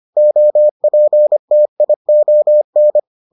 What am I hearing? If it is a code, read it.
Morse code, heard as a keyed tone, about 25 wpm, 600 Hz. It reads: OPTION